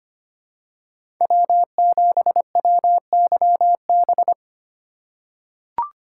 Morse code, heard as a keyed tone, about 25 wpm, 700 Hz. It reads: W7WYB
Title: USA Call Signs - 2x Character Spacing - 25wpm